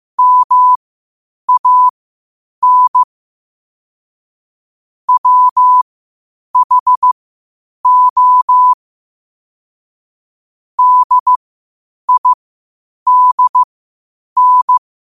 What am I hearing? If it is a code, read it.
Morse code, heard as a keyed tone, about 15 wpm, spaced as for 8 wpm, 1000 Hz. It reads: MAN WHO DIDN
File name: qotdfiles / 8wpm